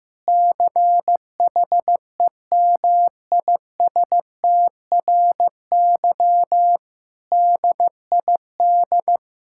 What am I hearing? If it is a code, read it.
Morse code, heard as a keyed tone, about 15 wpm, 700 Hz. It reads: CHEMISTRY DID
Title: Sentences from Top 1000 Words - Review Part 6 - 15wpm